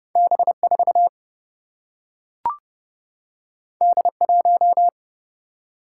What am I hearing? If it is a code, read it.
Morse code, heard as a keyed tone, about 31 wpm, 700 Hz. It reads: B4 D1
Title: Single-Letter Single-Number 30wpm